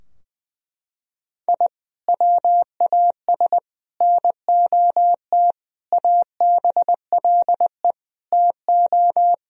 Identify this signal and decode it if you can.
Morse code, heard as a keyed tone, about 20 wpm, 700 Hz. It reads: I WAS NOT ABLE TO
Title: Sentences from Top 1000 Words - Review Part 2 - 20wpm